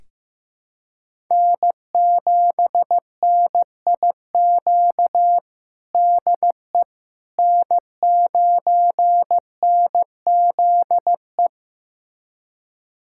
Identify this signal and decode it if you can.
Morse code, heard as a keyed tone, about 15 wpm, 700 Hz. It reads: N7NIQ DE N9NZE
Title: USA Callsign DE Callsign - 5 Characters - 15wpm